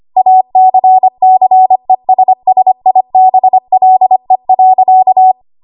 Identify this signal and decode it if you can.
Morse code, heard as a keyed tone, about 25 wpm, 750 Hz. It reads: ACCESSIBLE.
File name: Morse / 25WPM